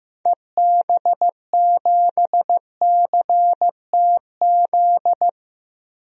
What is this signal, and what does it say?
Morse code, heard as a keyed tone, about 15 wpm, 700 Hz. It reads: EB7CTZ
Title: DXCC-100 Call Signs - 6 Characters - Repeated 3x - 15wpm